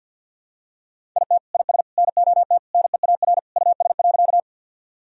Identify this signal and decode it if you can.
Morse code, heard as a keyed tone, about 46 wpm, 700 Hz. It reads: IT IS NOT NEAR US.
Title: Sentences from Top 1000 Words - Review Part 1 - 50wpm